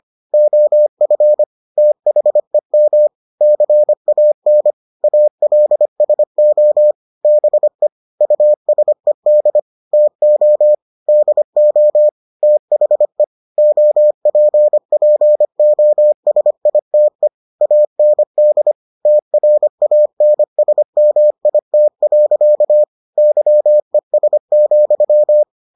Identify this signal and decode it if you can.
Morse code, heard as a keyed tone, about 25 wpm, 600 Hz. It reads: OF THEM CAN ALSO BE USED TO DO THE OPPOSITE AND TRANSMIT. YES,